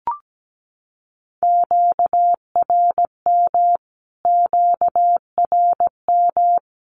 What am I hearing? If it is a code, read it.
Morse code, heard as a keyed tone, about 17 wpm, 700 Hz. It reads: QRM QRM
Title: Q-Codes Repeated 3x 17wpm